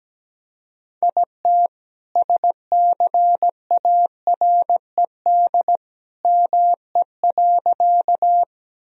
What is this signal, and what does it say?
Morse code, heard as a keyed tone, about 17 wpm, 700 Hz. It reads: IT SCARED ME.
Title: Sentences from Top 1000 Words - Review Part 4 - 17wpm